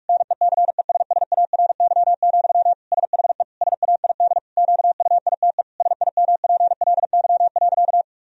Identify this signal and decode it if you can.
Morse code, heard as a keyed tone, about 42 wpm, 700 Hz. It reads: NECESSARY, SHE SAID QUITE SIMPLY.
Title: Sentences from Top 1000 Words - Review Part 4 - 45wpm